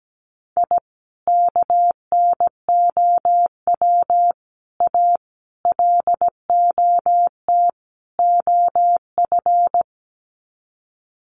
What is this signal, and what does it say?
Morse code, heard as a keyed tone, about 17 wpm, 700 Hz. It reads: I KNOW A LOT OF
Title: Sets of 5 Words Encore 17wpm